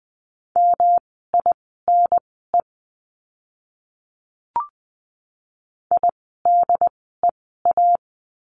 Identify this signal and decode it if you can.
Morse code, heard as a keyed tone, about 20 wpm, spaced as for 15 wpm, 700 Hz. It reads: MINE IDEA
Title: 108 Lesson 18f - Letter m - Words 20wpm